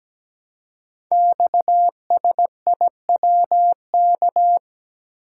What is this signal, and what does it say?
Morse code, heard as a keyed tone, about 17 wpm, 700 Hz. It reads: XSIWK
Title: Code Groups - 5 Letters - 17wpm